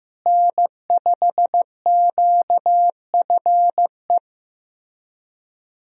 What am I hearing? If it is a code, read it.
Morse code, heard as a keyed tone, about 16 wpm, 700 Hz. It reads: N5QFE
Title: USA Call Signs - 5 Characters - Repeated 3x - 15wpm